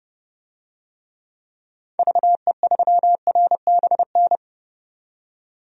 Morse code, heard as a keyed tone, about 30 wpm, 700 Hz. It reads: VE3RBN
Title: DXCC-100 Call Signs - 6 Characters - 30wpm